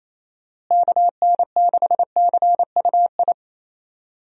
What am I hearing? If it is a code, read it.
Morse code, heard as a keyed tone, about 28 wpm, 700 Hz. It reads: KN6CUI
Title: USA Call Signs - 6 Characters - Repeated 3x - 28wpm